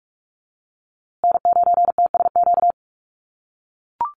Morse code, heard as a keyed tone, about 44 wpm, 700 Hz. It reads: N9TSQ
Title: USA Call Signs Repeated 2x 45wpm